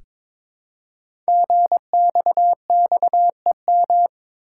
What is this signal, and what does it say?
Morse code, heard as a keyed tone, about 23 wpm, 700 Hz. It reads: GXXEM
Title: Code Groups - 5 Letters - 22wpm